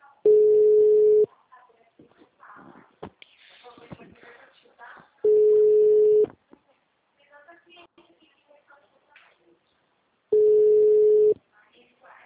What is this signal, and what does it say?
Ringback tone (Europe, 425 Hz), with speech.
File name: out- -2002- - - .52719